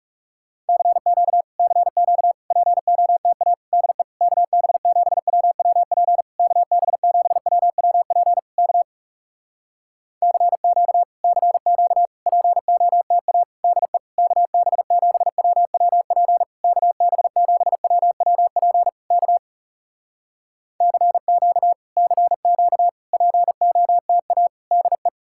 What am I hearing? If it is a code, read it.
Morse code, heard as a keyed tone, about 40 wpm, 700 Hz. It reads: CQ CQ POTA DE KB7WWP KB7WWP K CQ CQ POTA DE KB7WWP KB7WWP K CQ CQ POTA DE